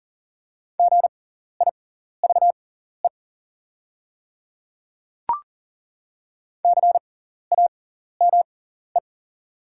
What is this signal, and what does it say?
Morse code, heard as a keyed tone, about 40 wpm, spaced as for 14 wpm, 690 Hz. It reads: GIVE CAME
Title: 6x Character Spacing - Top 200 Words - 40wpm